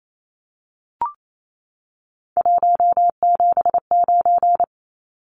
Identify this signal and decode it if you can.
Morse code, heard as a keyed tone, about 28 wpm, 700 Hz. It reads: 179
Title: Numbers - 3-Digit 28wpm